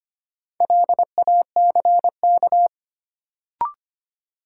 Morse code, heard as a keyed tone, about 25 wpm, 700 Hz. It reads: LACK
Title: Sentences from Top 1100 Words - New Word Drill Alt - 25wpm